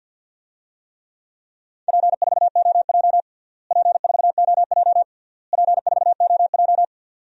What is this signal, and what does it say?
Morse code, heard as a keyed tone, about 48 wpm, 700 Hz. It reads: P4OJ P4OJ P4OJ
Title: DXCC-100 Call Signs - 4 Characters - Repeated 3x - 50wpm